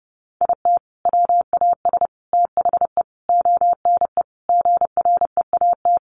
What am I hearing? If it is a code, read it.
Morse code, heard as a keyed tone, about 30 wpm, 700 Hz. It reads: IT WAS THE ONE GREAT
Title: Sentences from Top 1400 Words 30wpm